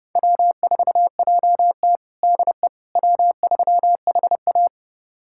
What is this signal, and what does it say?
Morse code, heard as a keyed tone, about 30 wpm, 700 Hz. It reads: W4JT DE W3HA
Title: USA Callsign DE Callsign - 4 Characters - 30wpm